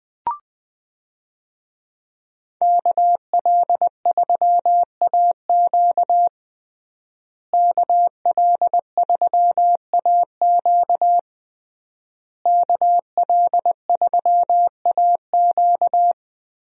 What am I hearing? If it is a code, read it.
Morse code, heard as a keyed tone, about 20 wpm, 700 Hz. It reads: KL3AQ KL3AQ KL3AQ
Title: USA Call Signs - 5 Characters - Repeated 3x - 20wpm